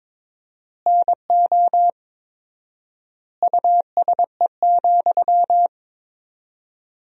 Morse code, heard as a keyed tone, about 22 wpm, 700 Hz. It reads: NO USE,
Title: 4x Word-Spacing - Sentences from Top 500 Words - 22wpm